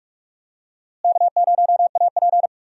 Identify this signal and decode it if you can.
Morse code, heard as a keyed tone, about 45 wpm, 700 Hz. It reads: K0AP
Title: USA Call Signs - 4 Characters - Repeated 2x - 45wpm